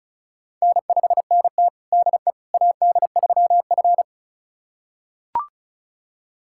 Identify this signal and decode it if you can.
Morse code, heard as a keyed tone, about 36 wpm, 700 Hz. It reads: N5NT DE AD3F
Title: USA Callsign DE Callsign - 4 Characters - 35wpm